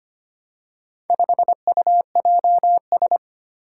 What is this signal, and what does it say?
Morse code, heard as a keyed tone, about 25 wpm, 700 Hz. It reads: 5UJS
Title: Four Challenging Letter-Number Combinations 25wpm